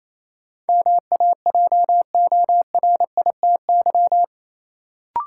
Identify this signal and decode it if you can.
Morse code, heard as a keyed tone, about 28 wpm, 700 Hz. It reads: MAJORITY
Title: Sets of 3 Words - New Word Drill - 28wpm